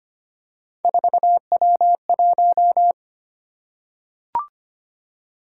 Morse code, heard as a keyed tone, about 25 wpm, 700 Hz. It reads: 4W1
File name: Three Letter-Number Combinations - 25wpm